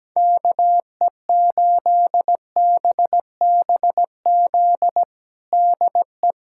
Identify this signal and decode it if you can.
Morse code, heard as a keyed tone, about 17 wpm, 700 Hz. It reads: KE8BBZ DE
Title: USA Callsign DE Callsign - 6 Characters - 17wpm